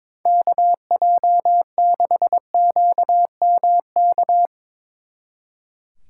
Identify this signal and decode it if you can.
Morse code, heard as a keyed tone, about 22 wpm, 700 Hz. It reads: KJ6QMK
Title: Speed Racing - USA 2x3 Call Signs 22wpm